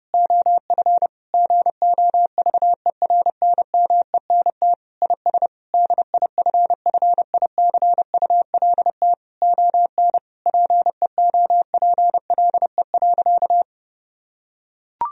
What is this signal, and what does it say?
Morse code, heard as a keyed tone, about 30 wpm, 700 Hz. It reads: OF GOVERNMENT IS DIFFICULT ON PEOPLE.
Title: Sentences from Top 1000 Words - Review Part 3 - 30wpm